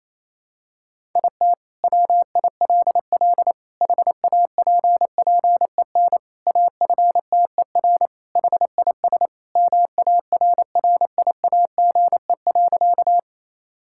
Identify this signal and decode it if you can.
Morse code, heard as a keyed tone, about 28 wpm, 700 Hz. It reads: IT WILL HAPPEN AFTER HIS MARRIAGE.
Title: Sentences from Top 1000 Words - Review Part 5 - 28wpm